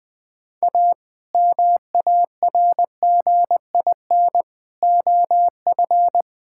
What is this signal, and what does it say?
Morse code, heard as a keyed tone, about 20 wpm, 700 Hz. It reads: A MARGIN OF
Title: Sets of 8 Words - 20wpm